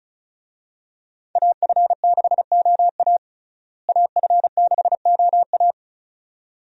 Morse code, heard as a keyed tone, about 34 wpm, 700 Hz. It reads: AF6OA AF6OA
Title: USA Call Signs - 5 Characters - Repeated 2x - 35wpm